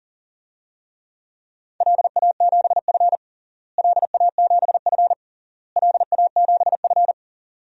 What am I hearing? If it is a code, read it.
Morse code, heard as a keyed tone, about 40 wpm, 700 Hz. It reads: LA7F LA7F LA7F
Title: DXCC-100 Call Signs - 4 Characters - Repeated 3x - 40wpm